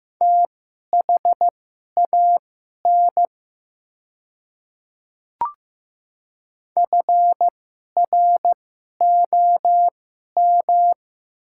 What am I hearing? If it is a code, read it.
Morse code, heard as a keyed tone, about 15 wpm, spaced as for 11 wpm, 700 Hz. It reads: THAN FROM
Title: 2x Character Spacing - Top 100 Words - 15wpm